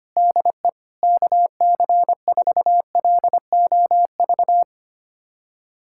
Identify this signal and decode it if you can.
Morse code, heard as a keyed tone, about 25 wpm, 700 Hz. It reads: DE KC4LOV
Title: USA Callsign DE Callsign 25wpm